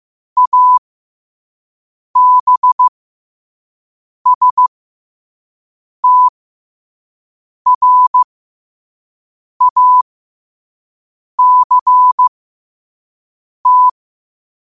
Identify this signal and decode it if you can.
Morse code, heard as a keyed tone, about 15 wpm, spaced as for 5 wpm, 1000 Hz. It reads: ABSTRACT
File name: qotdfiles / 5wpm